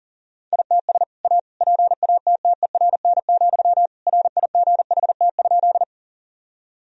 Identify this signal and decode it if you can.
Morse code, heard as a keyed tone, about 37 wpm, 700 Hz. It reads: ITS A PATTERN, RIGHT?
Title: Sentences from Top 1000 Words - Review Part 2 - 40wpm